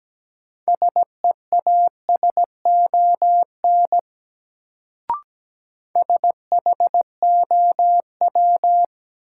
Morse code, heard as a keyed tone, about 17 wpm, 700 Hz. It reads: SEASON SHOW
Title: Sets of 5 Words - New Word Drill - 17wpm